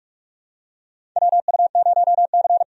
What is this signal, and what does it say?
Morse code, heard as a keyed tone, about 46 wpm, 700 Hz. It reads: WU0C
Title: USA Call Signs - 4 Characters - 45wpm